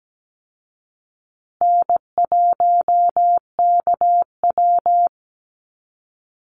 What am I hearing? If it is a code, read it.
Morse code, heard as a keyed tone, about 17 wpm, 700 Hz. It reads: N1KW
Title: Easy Call Signs Repeated 3x - 17wpm